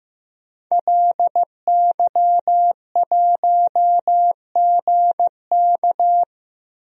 Morse code, heard as a keyed tone, about 15 wpm, 700 Hz. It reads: LY1GK
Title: DXCC-100 Call Signs - 5 Characters - 15wpm